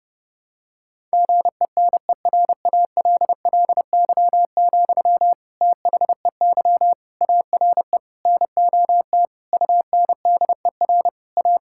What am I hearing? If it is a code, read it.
Morse code, heard as a keyed tone, about 32 wpm, 700 Hz. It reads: GENERALLY, THEY ARE NOT UNDER A
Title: Sentences from Top 1000 Words - Review Part 6 - 30wpm